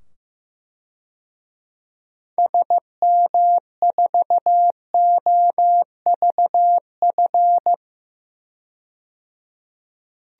Morse code, heard as a keyed tone, about 15 wpm, 700 Hz. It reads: SM4OVF
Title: DXCC-100 Call Signs - 6 Characters - 15wpm